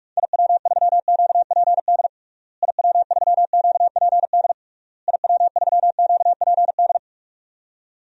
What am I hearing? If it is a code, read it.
Morse code, heard as a keyed tone, about 46 wpm, 700 Hz. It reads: IW3QPD IW3QPD IW3QPD
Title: DXCC-100 Call Signs - 6 Characters - Repeated 3x - 45wpm